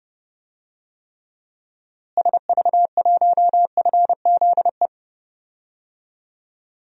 Morse code, heard as a keyed tone, about 30 wpm, 700 Hz. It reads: SV1FZE
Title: DXCC-100 Call Signs - 6 Characters - Repeated 3x - 30wpm